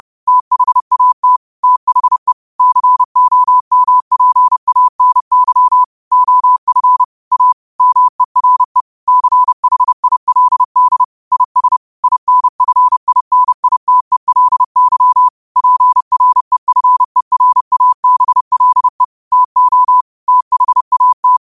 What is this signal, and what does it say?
Morse code, heard as a keyed tone, about 30 wpm, 1000 Hz. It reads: THAT THE COMPANY OF A MERE CHILD IS INFINITELY PREFERABLE TO THAT
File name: qotdfiles / 30wpm